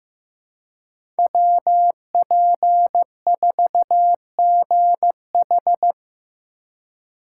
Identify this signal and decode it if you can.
Morse code, heard as a keyed tone, about 15 wpm, 700 Hz. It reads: WP4GH
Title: USA Call Signs - 2x Character Spacing - 15wpm